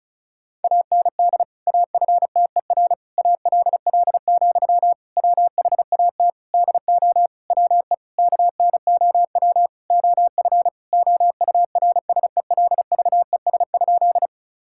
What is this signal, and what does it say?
Morse code, heard as a keyed tone, about 34 wpm, 700 Hz. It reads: AND AFTER ALL, WHAT DO WE KNOW OF OURSELVES?
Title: Sentences from Top 1600 Words 35wpm